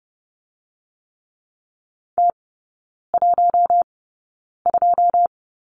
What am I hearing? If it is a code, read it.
Morse code, heard as a keyed tone, about 30 wpm, spaced as for 9 wpm, 700 Hz. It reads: T12
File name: Lesson 19e - Letter w - Three Characters 30wpm